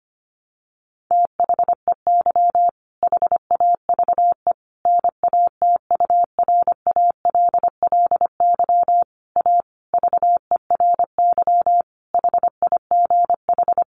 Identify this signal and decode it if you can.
Morse code, heard as a keyed tone, about 25 wpm, 700 Hz. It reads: THEY HAVE NATURALLY A VERY HIGH
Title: Sentences from Top 1300 Words 25wpm